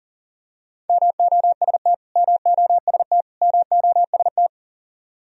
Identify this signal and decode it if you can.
Morse code, heard as a keyed tone, about 40 wpm, 700 Hz. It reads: MOST MOST MOST
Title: Top 100 Words Repeated 40wpm